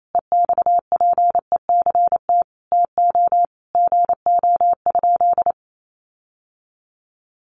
Morse code, heard as a keyed tone, about 28 wpm, 700 Hz. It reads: EXPECT TO GO?
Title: Sentences from Top 1100 Words 28wpm